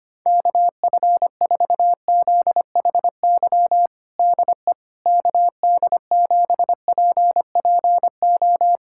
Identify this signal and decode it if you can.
Morse code, heard as a keyed tone, about 25 wpm, 700 Hz. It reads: KF4ZHY DE KD7PPO
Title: USA Callsign DE Callsign 25wpm